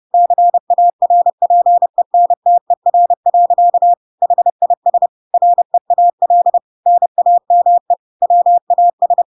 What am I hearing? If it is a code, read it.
Morse code, heard as a keyed tone, about 30 wpm, 700 Hz. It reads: CARPENTER. HIS REAL NAME WAS
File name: Pinocchio / Pinocchio ch1 30 30 ik1hge